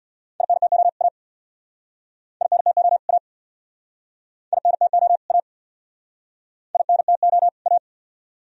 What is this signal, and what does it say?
Morse code, heard as a keyed tone, about 66 wpm, 700 Hz. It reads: INTO A INTO A INTO A INTO A